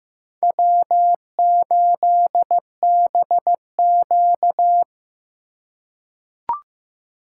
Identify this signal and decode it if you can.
Morse code, heard as a keyed tone, about 16 wpm, 700 Hz. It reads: W8BQ
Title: USA Call Signs - 3x Character Spacing - 15wpm